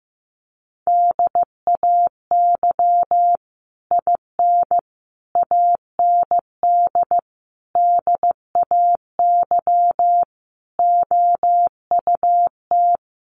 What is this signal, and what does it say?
Morse code, heard as a keyed tone, about 15 wpm, 700 Hz. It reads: DAY IN AND DAY OUT
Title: Sets of 5 Words Encore2 15wpm